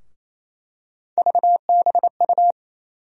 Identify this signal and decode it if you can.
Morse code, heard as a keyed tone, about 28 wpm, 700 Hz. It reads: VBU
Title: USA Call Sign Postfixes - 3 Letters 28wpm